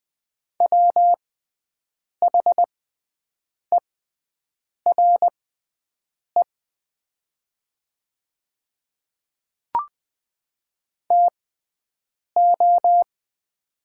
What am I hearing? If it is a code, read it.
Morse code, heard as a keyed tone, about 20 wpm, spaced as for 7 wpm, 700 Hz. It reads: WHERE TO
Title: 6x Character Spacing - Top 200 Words - 20wpm